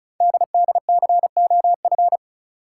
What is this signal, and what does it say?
Morse code, heard as a keyed tone, about 36 wpm, 700 Hz. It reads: DDCOF
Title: Code Groups - 5 Letters - 35wpm